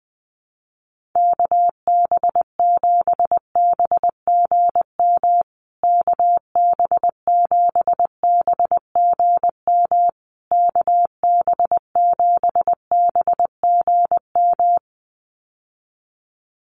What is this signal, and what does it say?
Morse code, heard as a keyed tone, about 20 wpm, 700 Hz. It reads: KB7BGM KB7BGM KB7BGM
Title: USA Call Signs Repeated 3x 20wpm